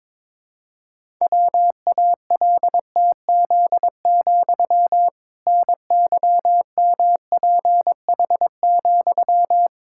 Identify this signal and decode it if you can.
Morse code, heard as a keyed tone, about 22 wpm, 700 Hz. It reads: WALTZ, NYMPH,
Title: Pangram Challenge v2 - 22wpm